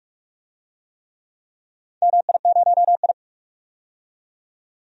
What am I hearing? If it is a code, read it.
Morse code, heard as a keyed tone, about 46 wpm, 700 Hz. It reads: MI0I
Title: DXCC-100 Call Signs - 4 Characters - 45wpm